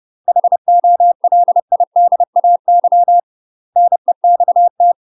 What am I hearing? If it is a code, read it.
Morse code, heard as a keyed tone, about 30 wpm, 700 Hz. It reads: HOLIDAY NEXT
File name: UK MASTER QSOs TRAINING / 30WPM / UK MASTER QSOs 0015